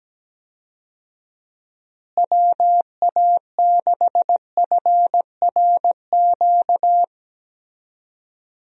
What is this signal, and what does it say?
Morse code, heard as a keyed tone, about 17 wpm, 700 Hz. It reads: WA6FRQ
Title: USA Call Signs - 6 Characters - Repeated 2x - 17wpm